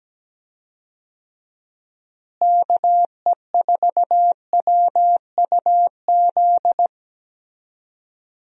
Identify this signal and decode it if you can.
Morse code, heard as a keyed tone, about 17 wpm, 700 Hz. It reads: KE4WUZ
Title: USA Call Signs - 6 Characters - Repeated 3x - 17wpm